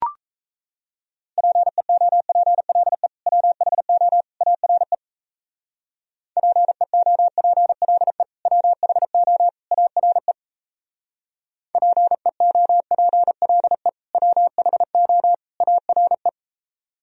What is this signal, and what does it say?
Morse code, heard as a keyed tone, about 34 wpm, 700 Hz. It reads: PEOPLE WHO ARE PEOPLE WHO ARE PEOPLE WHO ARE
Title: Speed-Racing - Sets of 3 Words 28wpm